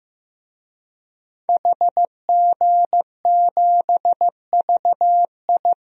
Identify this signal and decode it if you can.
Morse code, heard as a keyed tone, about 15 wpm, 700 Hz. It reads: HG7VI
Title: DXCC-100 Call Signs - 5 Characters - Repeated 3x - 15wpm